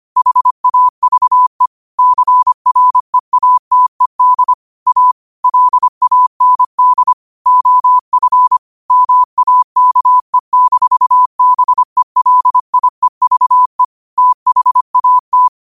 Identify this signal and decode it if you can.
Morse code, heard as a keyed tone, about 25 wpm, 1000 Hz. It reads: HAVE CREATED A LAND OF MAKE-BELIEVE THAT
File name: qotdfiles / 25wpm